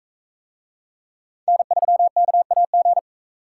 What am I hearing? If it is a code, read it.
Morse code, heard as a keyed tone, about 44 wpm, 700 Hz. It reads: N3KAG